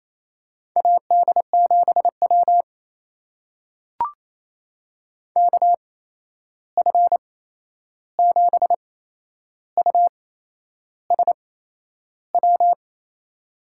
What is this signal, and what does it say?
Morse code, heard as a keyed tone, about 28 wpm, spaced as for 11 wpm, 700 Hz. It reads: AD7W KF7USW
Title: USA Call Signs - 8x Character Spacing - 28wpm